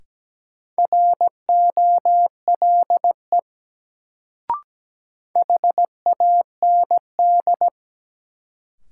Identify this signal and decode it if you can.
Morse code, heard as a keyed tone, about 17 wpm, 700 Hz. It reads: ROLE HAND
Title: Sets of 3 Words - New Word Drill - 17wpm